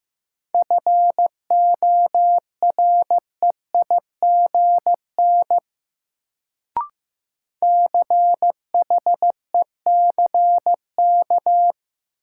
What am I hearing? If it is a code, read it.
Morse code, heard as a keyed tone, about 15 wpm, 700 Hz. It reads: FOREIGN CHECK
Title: Sentences from Top 1400 Words - New Word Drill Alt - 15wpm